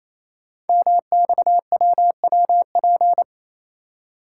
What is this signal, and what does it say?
Morse code, heard as a keyed tone, about 29 wpm, 700 Hz. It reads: MXWWP
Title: Code Groups - 5 Letters - 28wpm